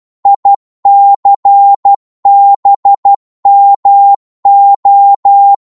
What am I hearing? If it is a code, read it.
Morse code, heard as a keyed tone, about 12 wpm, 800 Hz. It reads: ICBMO
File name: Morse / ltr-12wpm